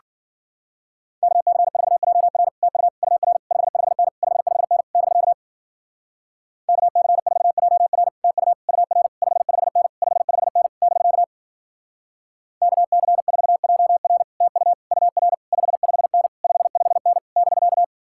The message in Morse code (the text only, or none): KC4JR TU UR 55N 55N <BK> KC4JR TU UR 55N 55N <BK> KC4JR TU UR 55N 55N <BK>